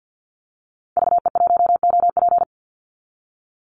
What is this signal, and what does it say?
Morse code, heard as a keyed tone, about 50 wpm, 700 Hz. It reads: VE1OP
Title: Call Signs Repeated 2x - 50wpm